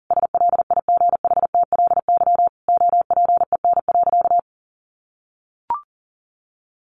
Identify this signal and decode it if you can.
Morse code, heard as a keyed tone, about 40 wpm, 700 Hz. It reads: SLIGHTLY OPEN.
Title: Sentences from Top 1300 Words 40wpm